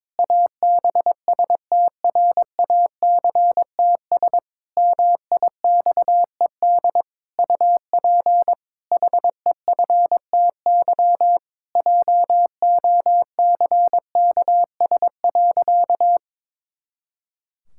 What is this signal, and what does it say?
Morse code, heard as a keyed tone, about 22 wpm, 700 Hz. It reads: ABSTRACTS MIXED UP HEFTY JOCKS.
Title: Pangram Challenge v2 - 22wpm